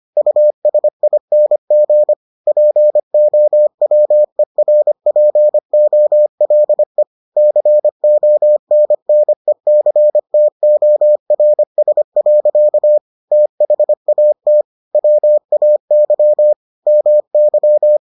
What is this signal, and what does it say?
Morse code, heard as a keyed tone, about 25 wpm, 600 Hz. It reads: USING POWERPOLE CONNECTORS. THAT WAY MY